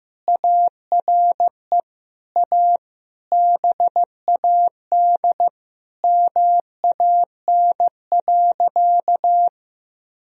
Morse code, heard as a keyed tone, about 15 wpm, 700 Hz. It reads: ARE A BAD MAN.
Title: Sentences from Top 1000 Words - Review Part 1 - 15wpm